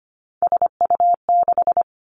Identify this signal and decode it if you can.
Morse code, heard as a keyed tone, about 25 wpm, 700 Hz. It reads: SU6
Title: Three Challenging Letter-Number Combinations 25wpm